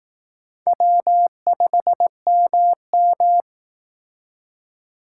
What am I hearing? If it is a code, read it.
Morse code, heard as a keyed tone, about 18 wpm, 700 Hz. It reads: W5MM
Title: Speed Racing - USA 1x2 Call Signs 15wpm